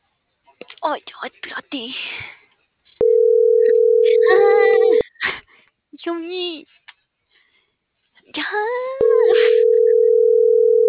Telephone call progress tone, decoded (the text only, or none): ringback tone